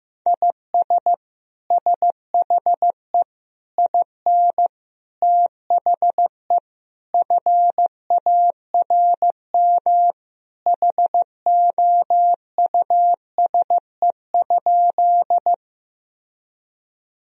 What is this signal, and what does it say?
Morse code, heard as a keyed tone, about 15 wpm, 700 Hz. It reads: IS SHE IN THE FARM HOUSE?
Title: Sentences from Top 1000 Words - Review Part 2 - 15wpm